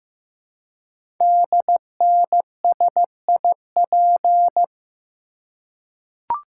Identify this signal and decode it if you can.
Morse code, heard as a keyed tone, about 15 wpm, 700 Hz. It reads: DNSIP
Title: Code Groups - 5 Letters - 15wpm